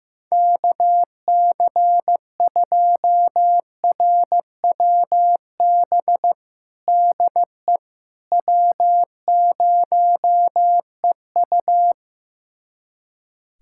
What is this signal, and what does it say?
Morse code, heard as a keyed tone, about 15 wpm, 700 Hz. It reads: KC2RWB DE W0EU
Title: USA Callsign DE Callsign 15wpm